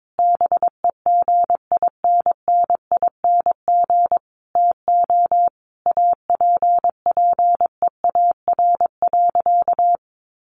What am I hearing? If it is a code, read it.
Morse code, heard as a keyed tone, about 22 wpm, 700 Hz. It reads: BEGINNING TO APPEAR.
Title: Sentences from Top 1200 Words 22wpm